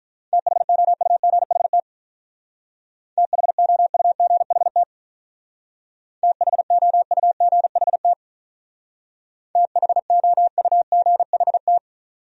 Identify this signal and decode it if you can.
Morse code, heard as a keyed tone, about 47 wpm, 700 Hz. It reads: THOUGHT THOUGHT THOUGHT THOUGHT